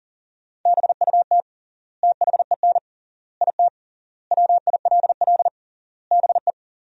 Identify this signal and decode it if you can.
Morse code, heard as a keyed tone, about 40 wpm, 700 Hz. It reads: BUT THEN IT WILL BE
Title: 3x Word-Spacing - Sentences from Top 300 Words - 40wpm